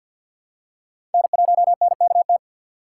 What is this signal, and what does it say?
Morse code, heard as a keyed tone, about 47 wpm, 700 Hz. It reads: N1NKT
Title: USA Call Signs - 5 Characters - 6x Character Spacing - 50wpm